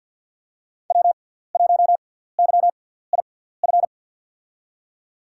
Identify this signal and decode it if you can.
Morse code, heard as a keyed tone, about 47 wpm, spaced as for 17 wpm, 700 Hz. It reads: W1YIF
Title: USA Call Signs - 5 Characters - 6x Character Spacing - 50wpm